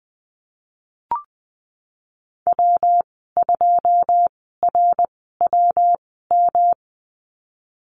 Morse code, heard as a keyed tone, about 19 wpm, spaced as for 14 wpm, 700 Hz. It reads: W2RWM
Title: 173 Lesson 28g - Number 9 - Call Signs 20wpm